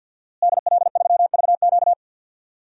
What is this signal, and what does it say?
Morse code, heard as a keyed tone, about 53 wpm, 700 Hz. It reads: DL3VQ
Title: DXCC-100 Call Signs - 5 Characters - Repeated 2x - 50wpm